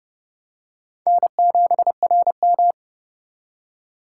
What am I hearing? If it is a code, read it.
Morse code, heard as a keyed tone, about 27 wpm, 700 Hz. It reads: N7RM